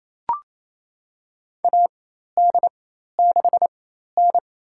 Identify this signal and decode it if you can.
Morse code, heard as a keyed tone, about 28 wpm, spaced as for 13 wpm, 700 Hz. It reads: AD6N
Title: USA Call Signs - 4 Characters - 4x Character Spacing - 28wpm